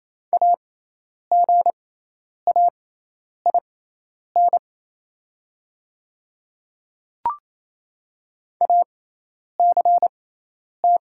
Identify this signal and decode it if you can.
Morse code, heard as a keyed tone, about 27 wpm, spaced as for 10 wpm, 700 Hz. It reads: AGAIN ACT
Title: 6x Character Spacing - Top 200 Words - 28wpm